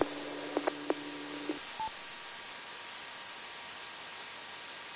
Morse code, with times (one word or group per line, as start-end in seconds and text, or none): none